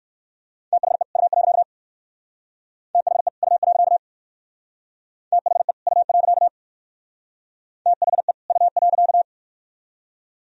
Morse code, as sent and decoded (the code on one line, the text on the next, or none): - .... / ..- .-.-.- / - .... / ..- .-.-.- / - .... / ..- .-.-.- / - .... / ..- .-.-.-
THE U. THE U. THE U. THE U.